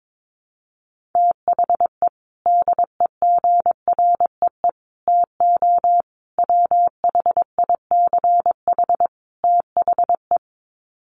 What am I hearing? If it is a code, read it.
Morse code, heard as a keyed tone, about 22 wpm, 700 Hz. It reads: THE DEGREE TO WHICH THE
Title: Sets of 5 Words Encore2 22wpm